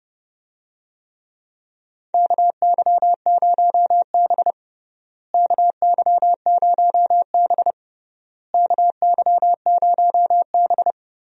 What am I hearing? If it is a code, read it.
Morse code, heard as a keyed tone, about 30 wpm, 700 Hz. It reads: KY0B KY0B KY0B
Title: USA Call Signs - 4 Characters - Repeated 3x - 30wpm